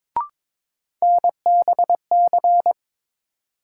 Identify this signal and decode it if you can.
Morse code, heard as a keyed tone, about 22 wpm, 700 Hz. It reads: NBC
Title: Sets of 5 Words Encore - New Word Drill - 22wpm